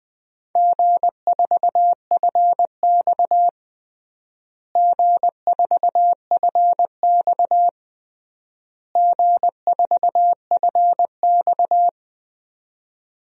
Morse code, heard as a keyed tone, about 20 wpm, 700 Hz. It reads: G4FX G4FX G4FX
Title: DXCC-100 Call Signs - 4 Characters - Repeated 3x - 20wpm